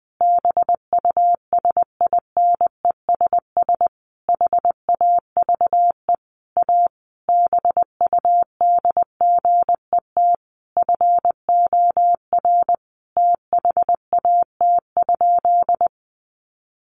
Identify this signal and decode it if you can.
Morse code, heard as a keyed tone, about 20 wpm, 700 Hz. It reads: BUSINESS HAVE A BUDGET FOR THAT?
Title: Sentences from Top 1200 Words 20wpm